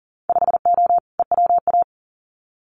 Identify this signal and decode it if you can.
Morse code, heard as a keyed tone, about 40 wpm, 700 Hz. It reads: 5O EWA
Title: Winter Field Day Exchanges Encore 40wpm